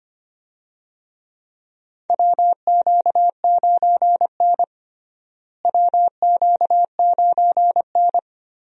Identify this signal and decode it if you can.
Morse code, heard as a keyed tone, about 25 wpm, 700 Hz. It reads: WQ9N WQ9N
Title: USA Call Signs - 4 Characters - Repeated 2x - 25wpm